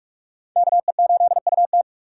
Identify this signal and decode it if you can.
Morse code, heard as a keyed tone, about 44 wpm, 700 Hz. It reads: KE8UT
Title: USA Call Signs - 5 Characters - 6x Character Spacing - 45wpm